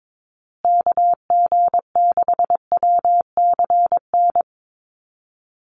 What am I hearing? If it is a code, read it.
Morse code, heard as a keyed tone, about 22 wpm, 700 Hz. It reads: KG6WCN
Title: USA Call Signs Repeated 3x 22wpm